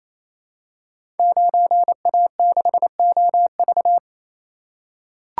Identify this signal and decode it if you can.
Morse code, heard as a keyed tone, about 29 wpm, 700 Hz. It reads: 9A6OV
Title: Code Groups - 5 Letter:Numbers - 28wpm